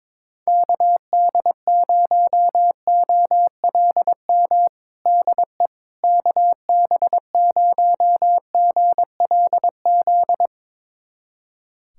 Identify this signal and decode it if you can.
Morse code, heard as a keyed tone, about 22 wpm, 700 Hz. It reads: KD0OLM DE KB0GLZ
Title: USA Callsign DE Callsign - 6 Characters - 22wpm